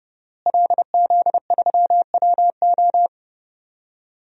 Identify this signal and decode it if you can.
Morse code, heard as a keyed tone, about 31 wpm, 700 Hz. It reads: LZ3WO
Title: DXCC-100 Call Signs - 5 Characters - Repeated 2x - 30wpm